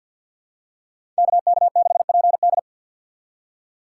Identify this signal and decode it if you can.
Morse code, heard as a keyed tone, about 48 wpm, 700 Hz. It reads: KK6PD
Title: USA Call Signs - 5 Characters - 6x Character Spacing - 50wpm